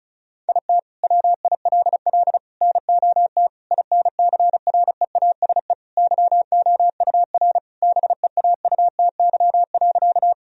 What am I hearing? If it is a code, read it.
Morse code, heard as a keyed tone, about 36 wpm, 700 Hz. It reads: IT WILL NOT INCREASE YOUR BEAUTY.
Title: Sentences from Top 1000 Words - Review Part 4 - 35wpm